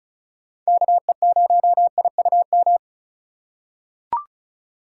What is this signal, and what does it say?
Morse code, heard as a keyed tone, about 36 wpm, 700 Hz. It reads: KE0IUM
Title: USA Call Signs - 6 Characters - 2x Character Spacing - 35wpm